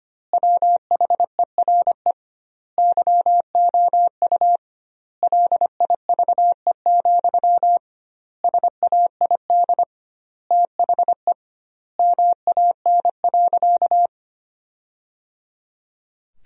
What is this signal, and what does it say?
Morse code, heard as a keyed tone, about 25 wpm, 700 Hz. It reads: WHERE YOU LIVE, SAID THE MAN.
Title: 2x Word-Spacing - Sentences from Top 200 Words - 25wpm